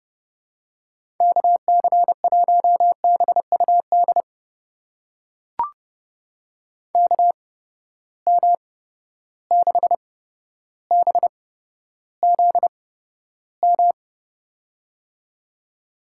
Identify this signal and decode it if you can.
Morse code, heard as a keyed tone, about 30 wpm, spaced as for 13 wpm, 700 Hz. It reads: KC1BUD KM6BZM
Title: USA Call Signs - 6 Characters - 8x Character Spacing - 30wpm